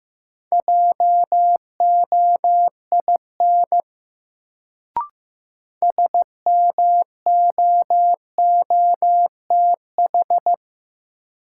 Morse code, heard as a keyed tone, about 15 wpm, 700 Hz. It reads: JOIN SMOOTH
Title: Sentences from Top 1400 Words - New Word Drill Alt - 15wpm